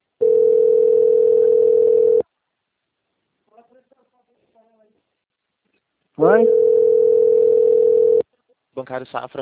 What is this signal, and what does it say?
Ringback tone (North America), 440 + 480 Hz, with speech.
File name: out- - - - - .68207